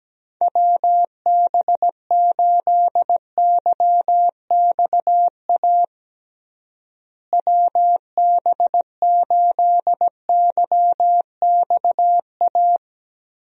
Morse code, heard as a keyed tone, about 17 wpm, 700 Hz. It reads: WB8YXA WB8YXA
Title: USA Call Signs - 6 Characters - Repeated 2x - 17wpm